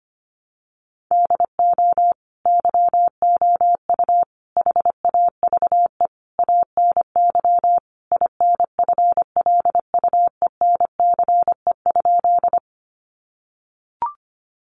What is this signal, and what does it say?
Morse code, heard as a keyed tone, about 25 wpm, 700 Hz. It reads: DO YOU HAVE ANY INFLUENCE?
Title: Sentences from Top 1100 Words 25wpm